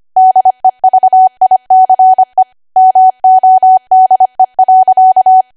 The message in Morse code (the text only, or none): DEVICE MODE.